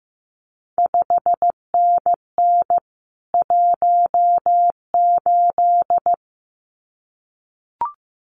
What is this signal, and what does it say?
Morse code, heard as a keyed tone, about 15 wpm, 700 Hz. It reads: 5NN 18
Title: CQ WW DX Contest Exchange Encore - 15wpm